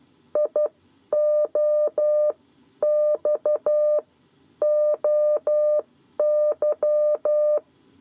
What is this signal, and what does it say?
Morse code, heard as a keyed tone, about 12 wpm, spaced as for 9 wpm, 600 Hz. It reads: IOXOY
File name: 7 Lezione cw 2